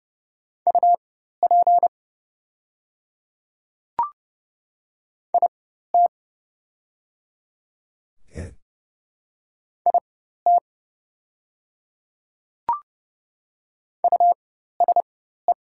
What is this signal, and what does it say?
Morse code, heard as a keyed tone, about 30 wpm, spaced as for 14 wpm, 700 Hz. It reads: UP IT IT USE
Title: 4x Character Spacing - Top 100 Words - 30wpm